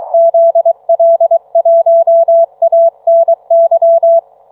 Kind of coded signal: Morse code